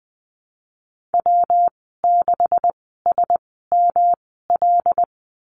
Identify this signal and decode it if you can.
Morse code, heard as a keyed tone, about 20 wpm, spaced as for 15 wpm, 700 Hz. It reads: W6SML
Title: 179 Lesson 29g - Slash - Call Signs 20wpm